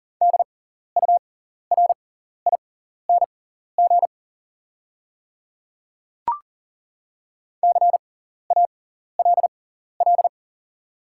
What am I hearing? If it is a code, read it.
Morse code, heard as a keyed tone, about 40 wpm, spaced as for 14 wpm, 690 Hz. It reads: DURING CALL
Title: 6x Character Spacing - Top 500 Words - 40wpm